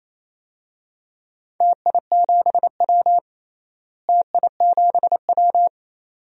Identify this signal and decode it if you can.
Morse code, heard as a keyed tone, about 27 wpm, 700 Hz. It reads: TI7W TI7W
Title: DXCC-100 Call Signs - 4 Characters - Repeated 2x - 28wpm